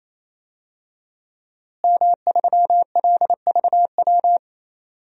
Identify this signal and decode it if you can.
Morse code, heard as a keyed tone, about 28 wpm, 700 Hz. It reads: M3LVW
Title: DXCC-100 Call Signs - 5 Characters - Repeated 3x - 28wpm